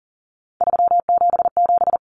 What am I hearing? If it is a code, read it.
Morse code, heard as a keyed tone, about 40 wpm, 700 Hz. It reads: 377
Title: Numbers - 3-Digit 40wpm